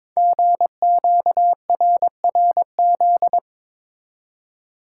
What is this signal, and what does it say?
Morse code, heard as a keyed tone, about 22 wpm, 700 Hz. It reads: GQRRZ
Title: Code Groups - 5 Letters - 22wpm